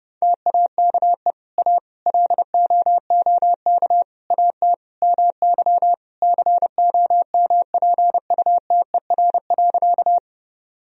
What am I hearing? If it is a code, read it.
Morse code, heard as a keyed tone, about 30 wpm, 700 Hz. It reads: TAKE A LOOK AT MY COMPUTER.
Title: Sentences from Top 1000 Words - Review Part 3 - 30wpm